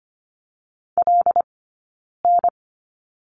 Morse code, heard as a keyed tone, about 25 wpm, 700 Hz. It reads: L N
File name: Lesson 15d - Number 5 - Two Characters 25wpm